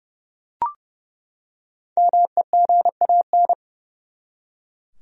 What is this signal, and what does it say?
Morse code, heard as a keyed tone, about 30 wpm, 700 Hz. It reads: MEGAN
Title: USA Common Names - 30wpm